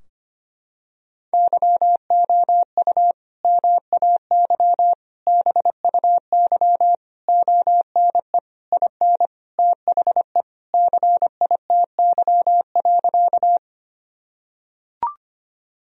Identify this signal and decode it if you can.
Morse code, heard as a keyed tone, about 25 wpm, 700 Hz. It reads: YOU MAY BUY ONE IN THE CITY.
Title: Sentences from Top 1000 Words - Review Part 4 - 25wpm